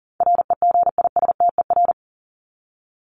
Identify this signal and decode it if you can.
Morse code, heard as a keyed tone, about 40 wpm, 700 Hz. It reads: REGISTER
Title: Top 1001-1250 Words 40wpm